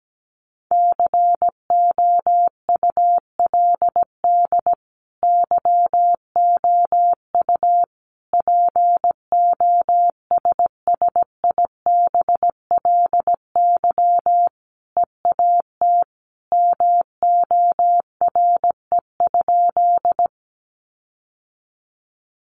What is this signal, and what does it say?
Morse code, heard as a keyed tone, about 17 wpm, 700 Hz. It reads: COULD YOU POSSIBLY EAT MORE?
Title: Sentences from Top 1300 Words 17wpm